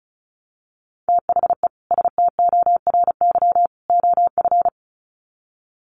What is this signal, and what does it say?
Morse code, heard as a keyed tone, about 34 wpm, 700 Hz. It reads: THE STORY OF
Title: Sets of 3 Words 35wpm